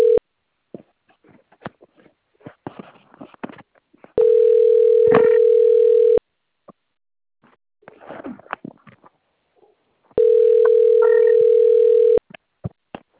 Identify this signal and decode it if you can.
Ringback tone (North America), 440 + 480 Hz.
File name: out- - - - - .211